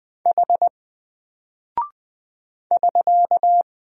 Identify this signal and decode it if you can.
Morse code, heard as a keyed tone, about 20 wpm, 700 Hz. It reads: H <SK>
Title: Single Character (Letter:Number:Punctuation:Prosign) Alt 20wpm